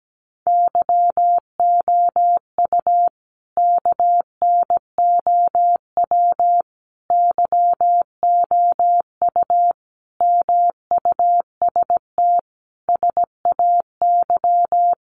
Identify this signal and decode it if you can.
Morse code, heard as a keyed tone, about 17 wpm, 700 Hz. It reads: YOU KNOW YOU MUST SAY
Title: Sentences from Top 500 Words 17wpm